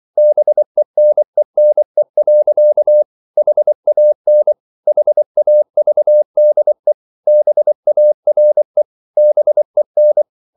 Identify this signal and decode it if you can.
Morse code, heard as a keyed tone, about 24 wpm, 600 Hz. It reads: BENENE. HAN HAVDE BARE BEN
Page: Ferie 24 wpm